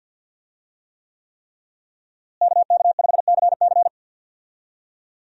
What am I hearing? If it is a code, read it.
Morse code, heard as a keyed tone, about 52 wpm, 700 Hz. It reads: KK5CC
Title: USA Call Signs - 5 Characters - Repeated 3x - 50wpm